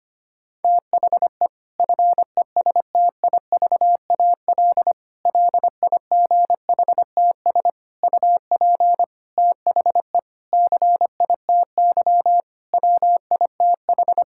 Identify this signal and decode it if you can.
Morse code, heard as a keyed tone, about 25 wpm, 700 Hz. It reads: THE FESTIVAL LIGHTS UP THE CITY WITH
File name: practice / 25wpm / Difficult Sentences with Hints - 25wpm